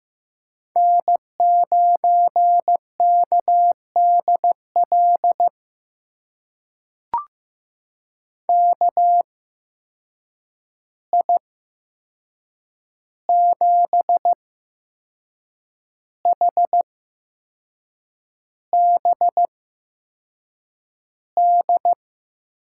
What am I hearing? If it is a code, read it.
Morse code, heard as a keyed tone, about 15 wpm, spaced as for 6 wpm, 700 Hz. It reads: N9KDL KI7HBD
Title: USA Call Signs - 8x Character Spacing - 15wpm